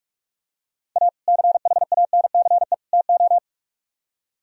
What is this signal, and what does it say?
Morse code, heard as a keyed tone, about 46 wpm, 700 Hz. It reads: A CHANCE TO